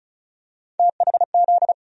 Morse code, heard as a keyed tone, about 34 wpm, 700 Hz. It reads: THZ
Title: USA Call Sign Postfixes - 3 Letters 35wpm